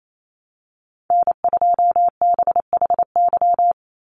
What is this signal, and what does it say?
Morse code, heard as a keyed tone, about 28 wpm, 700 Hz. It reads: N2BHY
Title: USA Call Signs 28wpm